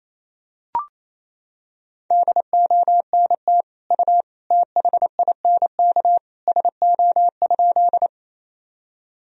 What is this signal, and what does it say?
Morse code, heard as a keyed tone, about 28 wpm, 700 Hz. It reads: DONT U THINK SO?
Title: Sentences from Top 300 Words and Abbreviations 28wpm